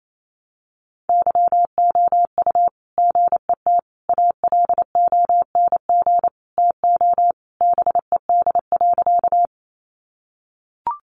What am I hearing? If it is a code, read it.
Morse code, heard as a keyed tone, about 28 wpm, 700 Hz. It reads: YOU GET ALONG TO BED.
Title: Sentences from Top 700 Words 28wpm